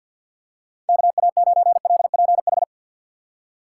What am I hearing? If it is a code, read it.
Morse code, heard as a keyed tone, about 52 wpm, 700 Hz. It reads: KA9LPH
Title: USA Call Signs - 3x Character Spacing - 50wpm